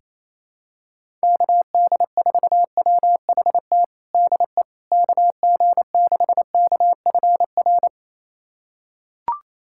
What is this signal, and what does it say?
Morse code, heard as a keyed tone, about 28 wpm, 700 Hz. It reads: KD4WHT DE KG6KFR
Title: USA Callsign DE Callsign - 6 Characters - 28wpm